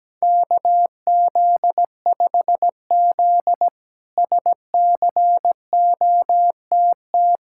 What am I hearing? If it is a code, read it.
Morse code, heard as a keyed tone, about 17 wpm, 700 Hz. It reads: KZ5Z SCOTT
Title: K1USN SST QSO Practice - 17wpm